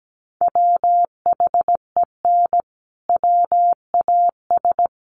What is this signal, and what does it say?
Morse code, heard as a keyed tone, about 17 wpm, 700 Hz. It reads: WHEN WAS
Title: Sentences from Top 1200 Words 17wpm